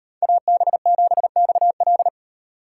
Morse code, heard as a keyed tone, about 38 wpm, 700 Hz. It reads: AB7XL